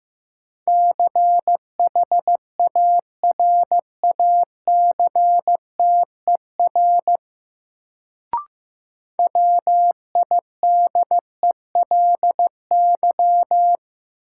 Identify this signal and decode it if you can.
Morse code, heard as a keyed tone, about 15 wpm, 700 Hz. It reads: CHARACTER WIDELY
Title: Sentences from Top 1200 Words - New Word Drill - 15wpm